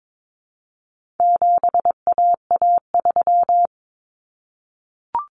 Morse code, heard as a keyed tone, about 22 wpm, 700 Hz. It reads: 7AA3
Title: Common Cut Numbers - 4-Digit 22wpm